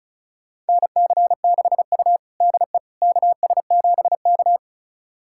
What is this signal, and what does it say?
Morse code, heard as a keyed tone, about 36 wpm, 700 Hz. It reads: NC6U DE KS7K
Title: USA Callsign DE Callsign - 4 Characters - 35wpm